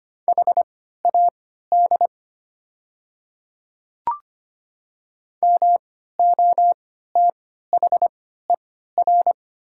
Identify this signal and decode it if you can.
Morse code, heard as a keyed tone, about 25 wpm, spaced as for 14 wpm, 700 Hz. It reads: HAD MOTHER
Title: 3x Character Spacing - Top 200 Words - 25wpm